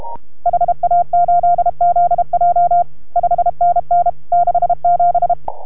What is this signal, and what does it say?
Morse code, heard as a keyed tone, about 31 wpm, 690 Hz. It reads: HA8ZJ 5NN 67